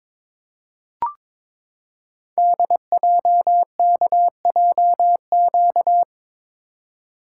Morse code, heard as a keyed tone, about 22 wpm, 700 Hz. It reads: DJKJQ
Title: Code Groups - 5 Letters - 22wpm